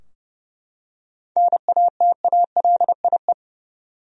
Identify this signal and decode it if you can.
Morse code, heard as a keyed tone, about 29 wpm, 700 Hz. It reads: NATALIE
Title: USA Common Names - 30wpm